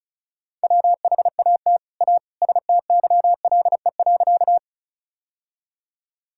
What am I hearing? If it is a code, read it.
Morse code, heard as a keyed tone, about 36 wpm, 700 Hz. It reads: WHAT A STYLE.
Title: Sentences from Top 1000 Words - Review Part 5 - 35wpm